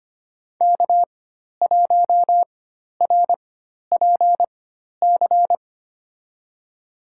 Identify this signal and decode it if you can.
Morse code, heard as a keyed tone, about 25 wpm, spaced as for 12 wpm, 700 Hz. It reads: K1RPC
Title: USA Call Signs - 4x Character Spacing - 25wpm